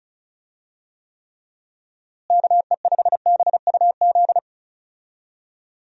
Morse code, heard as a keyed tone, about 36 wpm, 700 Hz. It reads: KE5BUZ
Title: USA Call Signs - 6 Characters - Repeated 2x - 35wpm